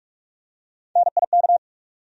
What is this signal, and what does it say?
Morse code, heard as a keyed tone, about 44 wpm, 700 Hz. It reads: NIK
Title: USA Call Sign Postfixes - 3 Letters 45wpm